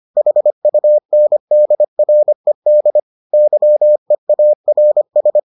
Morse code, heard as a keyed tone, about 25 wpm, 600 Hz. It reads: HUNDRED YEARS